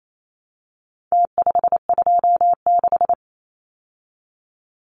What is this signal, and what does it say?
Morse code, heard as a keyed tone, about 28 wpm, 700 Hz. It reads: T526
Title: Numbers - Contest Style - 4-Digit 28wpm